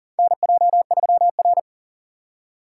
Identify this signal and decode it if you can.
Morse code, heard as a keyed tone, about 37 wpm, 700 Hz. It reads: NJ3R
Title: USA Call Signs - 3x Character Spacing - 40wpm